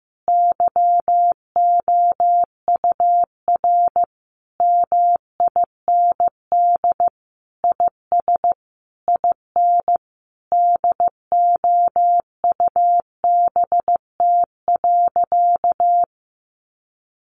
Morse code, heard as a keyed tone, about 15 wpm, 700 Hz. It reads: YOUR MIND IS IN DOUBT.
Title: Sentences from Top 1500 Words 15wpm